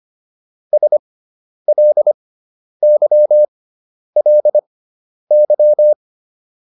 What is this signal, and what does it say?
Morse code, heard as a keyed tone, about 25 wpm, spaced as for 10 wpm, 600 Hz. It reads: SLYLY